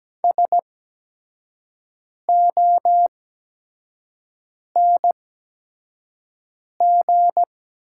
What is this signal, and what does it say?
Morse code, heard as a keyed tone, about 17 wpm, spaced as for 5 wpm, 700 Hz. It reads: SONG
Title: 8x Character Spacing - Top 500 Words - 17wpm